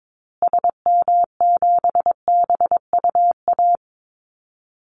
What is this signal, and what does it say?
Morse code, heard as a keyed tone, about 22 wpm, 700 Hz. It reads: SM7BUA
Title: Call Signs Repeated 2x - 22wpm